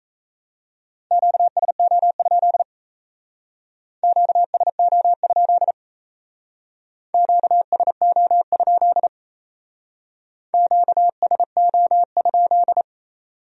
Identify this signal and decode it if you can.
Morse code, heard as a keyed tone, about 36 wpm, 700 Hz. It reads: QSO? QSO? QSO? QSO?